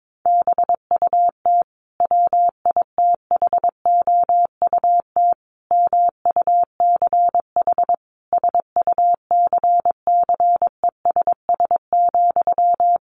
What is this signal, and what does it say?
Morse code, heard as a keyed tone, about 22 wpm, 700 Hz. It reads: BUT WITHOUT MUCH SUCCESS,
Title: Sentences from Top 900 Words 22wpm